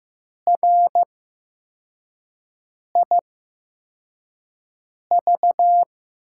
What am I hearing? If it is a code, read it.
Morse code, heard as a keyed tone, about 15 wpm, spaced as for 4 wpm, 700 Hz. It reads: RIV
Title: USA Call Signs - 5 Characters - 8x Character Spacing - 15wpm